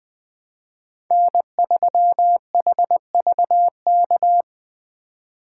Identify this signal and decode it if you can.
Morse code, heard as a keyed tone, about 20 wpm, 700 Hz. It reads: N3HVK